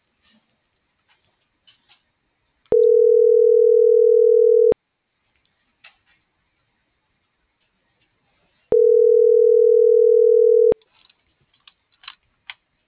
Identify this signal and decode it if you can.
Ringback tone (North America), 440 + 480 Hz.